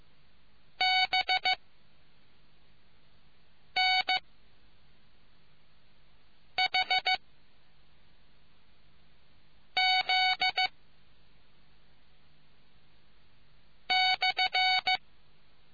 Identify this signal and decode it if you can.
Morse code, heard as a keyed tone, about 15 wpm, spaced as for 3 wpm, 750 Hz. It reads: BNHZ/